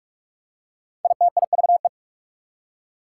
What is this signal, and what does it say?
Morse code, heard as a keyed tone, about 44 wpm, 690 Hz. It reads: ITIVE
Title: Prefixes and Suffixes 45wpm